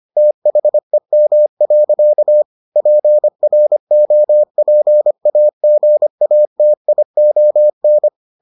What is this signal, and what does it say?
Morse code, heard as a keyed tone, about 25 wpm, 600 Hz. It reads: THEM. PROPAGATION